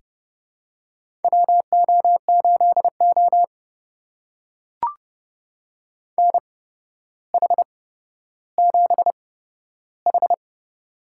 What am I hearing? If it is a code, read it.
Morse code, heard as a keyed tone, about 32 wpm, spaced as for 13 wpm, 700 Hz. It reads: WO8O NH7H
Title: USA Call Signs - 4 Characters - 8x Character Spacing - 30wpm